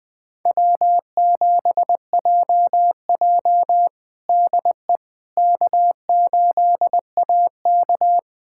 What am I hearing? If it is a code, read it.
Morse code, heard as a keyed tone, about 21 wpm, 700 Hz. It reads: W7JJ DE K8AK
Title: USA Callsign DE Callsign - 4 Characters - 20wpm